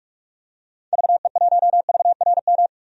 Morse code, heard as a keyed tone, about 42 wpm, 700 Hz. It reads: VE1VRM
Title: DXCC-100 Call Signs - 6 Characters - 45wpm